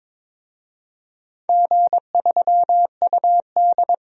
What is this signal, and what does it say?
Morse code, heard as a keyed tone, about 22 wpm, 700 Hz. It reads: G3UD
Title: DXCC-100 Call Signs - 4 Characters - 22wpm